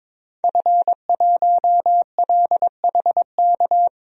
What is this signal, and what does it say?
Morse code, heard as a keyed tone, about 22 wpm, 700 Hz. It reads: F1LHK
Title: DXCC-100 Call Signs - 5 Characters - Repeated 2x - 22wpm